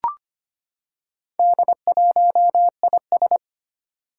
Morse code, heard as a keyed tone, about 25 wpm, 700 Hz. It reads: D1IS
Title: Four Challenging Letter-Number Combinations 25wpm